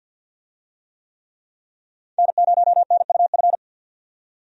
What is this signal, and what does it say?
Morse code, heard as a keyed tone, about 51 wpm, 700 Hz. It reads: N0NUF
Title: USA Call Signs - 5 Characters - 50wpm